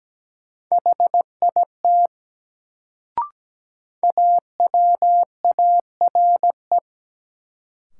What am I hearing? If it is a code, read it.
Morse code, heard as a keyed tone, about 17 wpm, 700 Hz. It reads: HIT AWARE
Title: Sentences from Top 1200 Words - New Word Drill - 17wpm